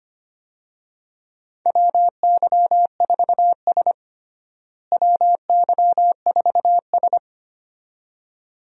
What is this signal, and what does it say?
Morse code, heard as a keyed tone, about 25 wpm, 700 Hz. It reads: WY4S WY4S
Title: USA Call Signs - 4 Characters - Repeated 2x - 25wpm